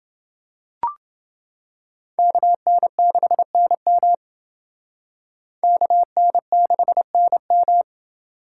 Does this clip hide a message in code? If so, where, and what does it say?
Yes: Morse code from 2.2 to 7.8 s, heard as a keyed tone, about 29 wpm, 700 Hz. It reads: KN6NM KN6NM